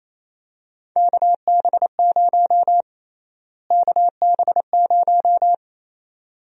Morse code, heard as a keyed tone, about 28 wpm, 700 Hz. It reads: KB0 KB0
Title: USA Call Sign Prefixes - Repeated 2x - 28wpm